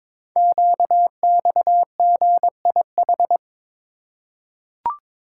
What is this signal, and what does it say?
Morse code, heard as a keyed tone, about 22 wpm, 700 Hz. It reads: QXGIH
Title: Code Groups - 5 Letters - 22wpm